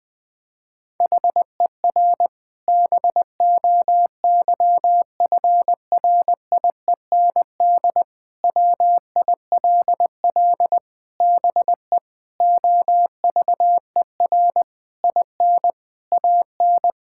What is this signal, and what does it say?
Morse code, heard as a keyed tone, about 21 wpm, 700 Hz. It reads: HER BOYFRIEND WILL BE OVER IN AN
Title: Sentences from Top 1000 Words - Review Part 5 - 20wpm